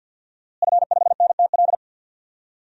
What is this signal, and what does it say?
Morse code, heard as a keyed tone, about 51 wpm, 700 Hz. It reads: F5NTL
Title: DXCC-100 Call Signs - 5 Characters - Repeated 2x - 50wpm